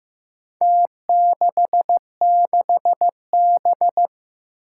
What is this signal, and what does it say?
Morse code, heard as a keyed tone, about 15 wpm, 700 Hz. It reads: T66B
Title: Four Challenging Letter-Number Combinations 15wpm